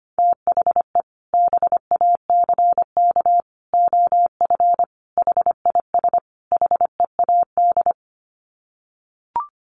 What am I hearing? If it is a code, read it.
Morse code, heard as a keyed tone, about 25 wpm, 700 Hz. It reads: THE BACK OF HIS HEAD
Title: Sets of 5 Words Encore 25wpm